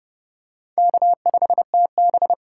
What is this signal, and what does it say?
Morse code, heard as a keyed tone, about 30 wpm, 700 Hz. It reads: K5TB
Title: USA Call Signs - 8x Character Spacing - 30wpm